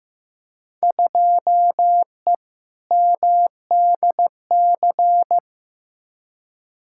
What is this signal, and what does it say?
Morse code, heard as a keyed tone, about 15 wpm, 700 Hz. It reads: 2E MDC
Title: Field Day Exchanges - 15wpm